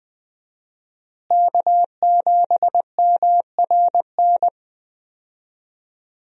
Morse code, heard as a keyed tone, about 20 wpm, 700 Hz. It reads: K7MRN
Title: USA Call Signs - 5 Characters - Repeated 3x - 20wpm